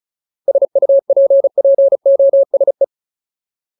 Morse code, heard as a keyed tone, about 34 wpm, 550 Hz. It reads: SUPPOSE
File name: Ultimate Word List / 35WPM Ultimate